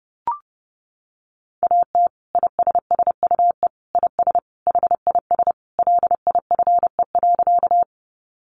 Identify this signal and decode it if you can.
Morse code, heard as a keyed tone, about 30 wpm, 700 Hz. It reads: AT ISSUE IS HIS LIFE.
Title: Sentences from Top 900 Words 30wpm